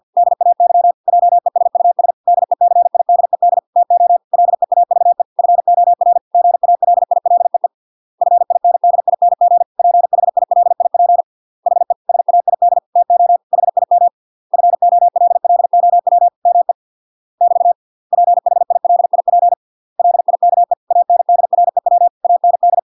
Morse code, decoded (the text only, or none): DAY JESUS DECIDED TO LEAVE FOR GALILEE FINDING PHILIP HE SAID TO HIM FOLLOW ME <BT> PHILIP LIKE ANDREW AND